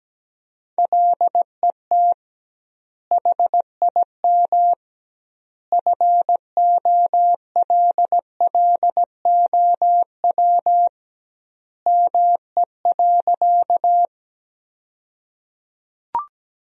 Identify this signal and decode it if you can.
Morse code, heard as a keyed tone, about 17 wpm, 700 Hz. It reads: LET HIM FOLLOW ME.
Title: 2x Word-Spacing - Sentences from Top 300 Words - 17wpm